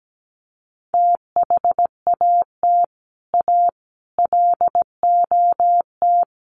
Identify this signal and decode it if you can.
Morse code, heard as a keyed tone, about 17 wpm, 700 Hz. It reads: THAT A LOT
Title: Sets of 4 Words 17wpm